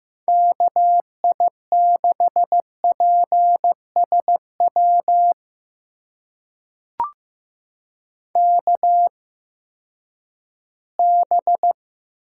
Morse code, heard as a keyed tone, about 15 wpm, spaced as for 10 wpm, 700 Hz. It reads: KI6PSW KB
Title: USA Call Signs - 6 Characters - 8x Character Spacing - 15wpm